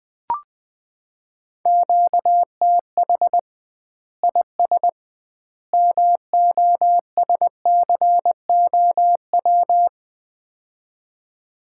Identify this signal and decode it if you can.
Morse code, heard as a keyed tone, about 20 wpm, 700 Hz. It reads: QTH IS MOSCOW
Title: Common QSO Phrases - 2x Word-Spacing - 20wpm